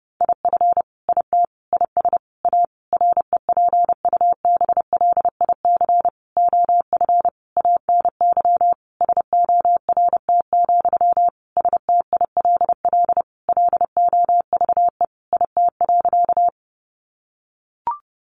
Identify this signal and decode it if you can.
Morse code, heard as a keyed tone, about 30 wpm, 700 Hz. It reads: IF IT IS A REPUBLIC OF ANY SORT, STILL LOVE IT.
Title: Sentences from Top 1500 Words 30wpm